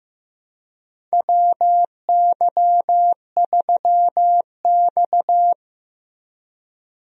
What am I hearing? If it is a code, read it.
Morse code, heard as a keyed tone, about 15 wpm, 700 Hz. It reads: WY3X
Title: USA Call Signs - 4 Characters - Repeated 2x - 15wpm